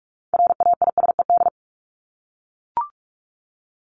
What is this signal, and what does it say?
Morse code, heard as a keyed tone, about 44 wpm, 700 Hz. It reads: RAISED
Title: ED Suffix - Top 100 Words with Base Words 45wpm